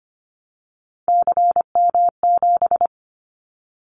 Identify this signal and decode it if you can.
Morse code, heard as a keyed tone, about 25 wpm, 700 Hz. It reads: CM7
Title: Callsign Prefixes for Top 100 DXCC - 25wpm